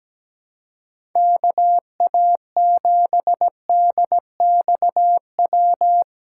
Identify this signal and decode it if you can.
Morse code, heard as a keyed tone, about 17 wpm, 700 Hz. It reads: KA7DXW
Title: USA Callsign DE Callsign 17wpm